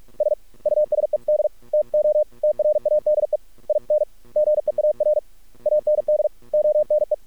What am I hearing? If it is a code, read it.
Morse code, heard as a keyed tone, about 46 wpm, 600 Hz. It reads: S USED TO TRADE IN GEAR AND ONE
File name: archived / CRSnet